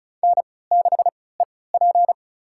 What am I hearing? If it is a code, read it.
Morse code, heard as a keyed tone, about 34 wpm, spaced as for 20 wpm, 700 Hz. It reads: N6EP
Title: USA Call Signs - 4 Characters - 3x Character Spacing - 35wpm